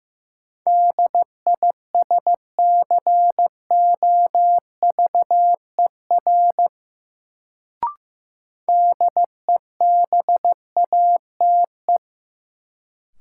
Morse code, heard as a keyed tone, about 15 wpm, 700 Hz. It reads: DISCOVER DEBATE
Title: Sentences from Top 1100 Words - New Word Drill - 15wpm